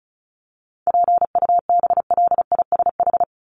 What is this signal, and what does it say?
Morse code, heard as a keyed tone, about 34 wpm, 700 Hz. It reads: PUBLISH
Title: ED Suffix - Top 100 Words with Base Words 35wpm